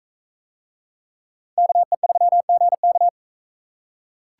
Morse code, heard as a keyed tone, about 41 wpm, 700 Hz. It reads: KE3GK